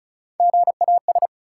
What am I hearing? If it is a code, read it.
Morse code, heard as a keyed tone, about 34 wpm, 700 Hz. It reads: GAS
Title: Sentences from Top 1100 Words - New Word Drill - 35wpm